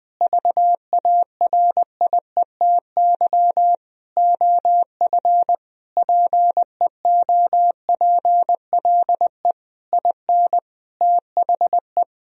Morse code, heard as a keyed tone, about 20 wpm, 700 Hz. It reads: VARIETY OF PEOPLE IN THE
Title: Sentences from Top 1000 Words - Review Part 4 - 20wpm